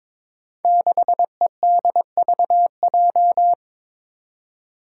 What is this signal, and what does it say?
Morse code, heard as a keyed tone, about 22 wpm, 700 Hz. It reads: 6EDVJ
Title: Code Groups - 5 Letter:Numbers - 22wpm